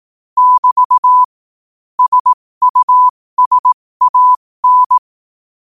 Morse code, heard as a keyed tone, about 18 wpm, spaced as for 15 wpm, 1000 Hz. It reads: <BT> SUSAN
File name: qotdfiles / 15wpm